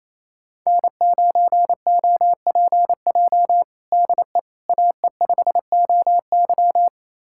Morse code, heard as a keyed tone, about 28 wpm, 700 Hz. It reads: N9OPJ DE AE5OY
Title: USA Callsign DE Callsign - 5 Characters - 28wpm